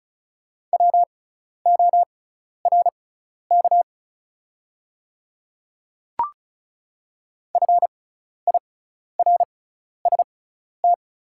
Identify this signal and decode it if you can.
Morse code, heard as a keyed tone, about 36 wpm, spaced as for 12 wpm, 690 Hz. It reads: WORK FIRST
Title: 6x Character Spacing - Top 300 Words - 35wpm